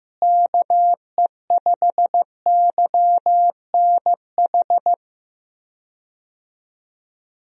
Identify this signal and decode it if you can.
Morse code, heard as a keyed tone, about 15 wpm, 700 Hz. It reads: KE5YNH
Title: USA Call Signs - 6 Characters - Repeated 3x - 15wpm